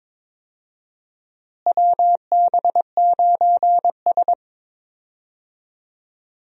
Morse code, heard as a keyed tone, about 21 wpm, 700 Hz. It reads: WB9S
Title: USA Call Signs - 4 Characters - Repeated 2x - 22wpm